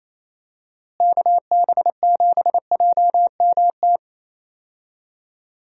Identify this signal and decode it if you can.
Morse code, heard as a keyed tone, about 28 wpm, 700 Hz. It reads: KB7JMT
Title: USA Call Signs - 6 Characters - Repeated 3x - 28wpm